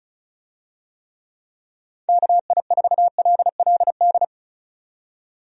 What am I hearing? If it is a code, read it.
Morse code, heard as a keyed tone, about 34 wpm, 700 Hz. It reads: KI4LLD
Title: USA Call Signs - 6 Characters - 35wpm